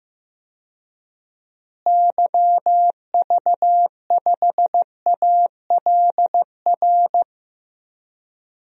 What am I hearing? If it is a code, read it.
Morse code, heard as a keyed tone, about 16 wpm, 700 Hz. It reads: YV5ALR
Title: DXCC-100 Call Signs - 6 Characters - Repeated 2x - 15wpm